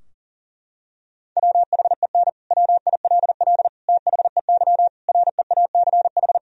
Morse code, heard as a keyed tone, about 40 wpm, 700 Hz. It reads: WHEN WILL THEY REACH
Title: Sentences from Top 1000 Words - Review Part 4 - 40wpm